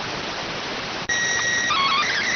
SSTV (slow-scan television), PD 290.